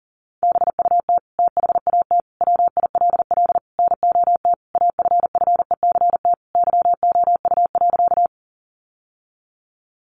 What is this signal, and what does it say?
Morse code, heard as a keyed tone, about 40 wpm, 700 Hz. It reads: BUT THAT WILL NOT AFFECT YOU.
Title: Sentences from Top 900 Words 40wpm